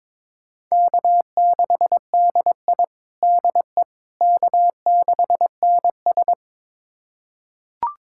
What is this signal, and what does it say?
Morse code, heard as a keyed tone, about 22 wpm, 700 Hz. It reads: K6DI DE K6NS
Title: USA Callsign DE Callsign - 4 Characters - 22wpm